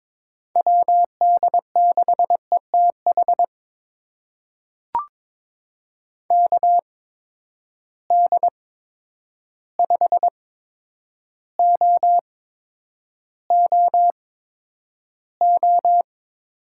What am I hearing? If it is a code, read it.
Morse code, heard as a keyed tone, about 22 wpm, spaced as for 9 wpm, 700 Hz. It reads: WD6ETH KD5OOO
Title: USA Call Signs - 6 Characters - 8x Character Spacing - 22wpm